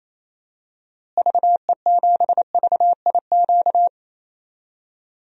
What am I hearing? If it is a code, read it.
Morse code, heard as a keyed tone, about 28 wpm, 700 Hz. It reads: VE7VIQ
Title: DXCC-100 Call Signs - 6 Characters - Repeated 2x - 28wpm